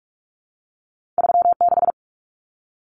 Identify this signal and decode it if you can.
Morse code, heard as a keyed tone, about 46 wpm, 700 Hz. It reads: 36
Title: Common Cut Numbers - 2-Digit 45wpm